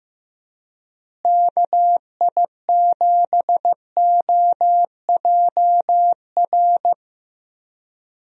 Morse code, heard as a keyed tone, about 16 wpm, 700 Hz. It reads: KI7OJR
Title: USA Call Signs - 6 Characters - Repeated 2x - 15wpm